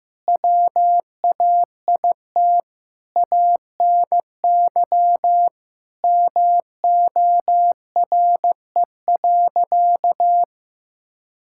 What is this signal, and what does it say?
Morse code, heard as a keyed tone, about 16 wpm, 700 Hz. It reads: WAIT ANY MORE.
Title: Sentences from Top 1000 Words - Review Part 5 - 15wpm